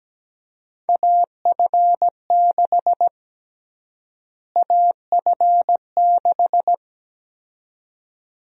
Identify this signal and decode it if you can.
Morse code, heard as a keyed tone, about 17 wpm, 700 Hz. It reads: AF6 AF6
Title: USA Call Sign Prefixes - Repeated 2x - 17wpm